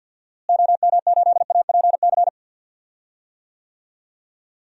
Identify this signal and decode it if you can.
Morse code, heard as a keyed tone, about 50 wpm, 700 Hz. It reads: OM8APC
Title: DXCC-100 Call Signs - 6 Characters - Repeated 3x - 50wpm